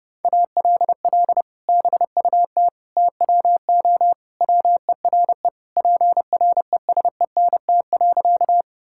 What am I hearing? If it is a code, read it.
Morse code, heard as a keyed tone, about 30 wpm, 700 Hz. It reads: ALL BUT TWO WERE PRESENT.
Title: Sentences from Top 1000 Words - Review Part 5 - 30wpm